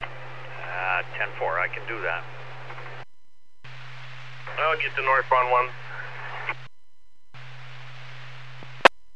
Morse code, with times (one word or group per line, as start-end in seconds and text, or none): none